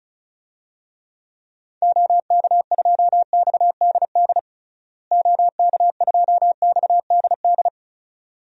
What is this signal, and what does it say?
Morse code, heard as a keyed tone, about 36 wpm, 700 Hz. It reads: OK2XDD OK2XDD
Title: DXCC-100 Call Signs - 6 Characters - Repeated 2x - 35wpm